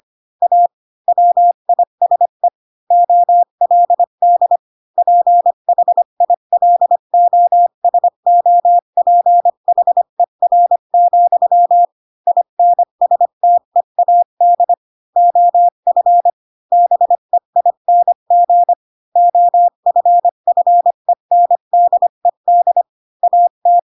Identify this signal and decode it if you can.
Morse code, heard as a keyed tone, about 25 wpm, 700 Hz. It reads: A WISE OLD PHILOSOPHER, INSTEAD OF BEING OFFENDED AT
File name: Pinocchio / Pinocchio ch4 20 25 ik1hge